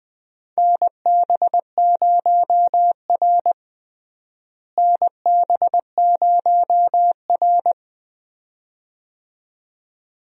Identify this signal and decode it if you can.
Morse code, heard as a keyed tone, about 20 wpm, 700 Hz. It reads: NB0R NB0R
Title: USA Call Signs - 4 Characters - Repeated 2x - 20wpm